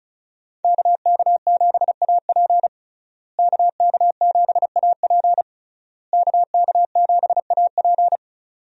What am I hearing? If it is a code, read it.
Morse code, heard as a keyed tone, about 35 wpm, 700 Hz. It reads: KK7AP KK7AP KK7AP
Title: USA Call Signs - 5 Characters - Repeated 3x - 35wpm